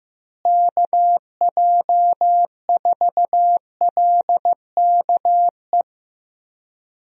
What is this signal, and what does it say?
Morse code, heard as a keyed tone, about 15 wpm, 700 Hz. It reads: KJ4LKE
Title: USA Call Signs - 6 Characters - Repeated 2x - 15wpm